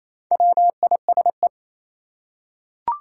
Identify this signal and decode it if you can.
Morse code, heard as a keyed tone, about 27 wpm, 700 Hz. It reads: WISE
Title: Four Letter Words - 28wpm